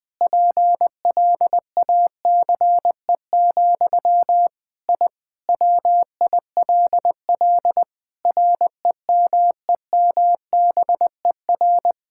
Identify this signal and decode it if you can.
Morse code, heard as a keyed tone, about 21 wpm, 700 Hz. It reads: PLACE, I WILL REMEMBER
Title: Sentences from Top 1000 Words - Review Part 5 - 20wpm